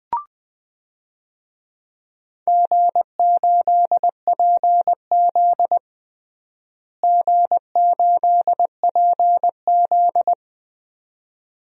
Morse code, heard as a keyed tone, about 20 wpm, 700 Hz. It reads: G8PZ G8PZ
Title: DXCC-100 Call Signs - 4 Characters - Repeated 2x - 20wpm